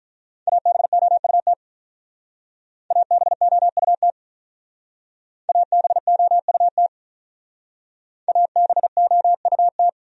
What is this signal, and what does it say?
Morse code, heard as a keyed tone, about 43 wpm, 700 Hz. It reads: ABOUT ABOUT ABOUT ABOUT
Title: Speed-Racing Top 500 Words 35wpm